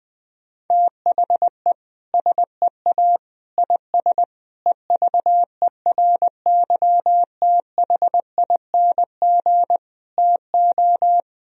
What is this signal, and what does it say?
Morse code, heard as a keyed tone, about 20 wpm, 700 Hz. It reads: THE SEA IS EVERYTHING TO
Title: Sentences from Top 1000 Words - Review Part 1 - 20wpm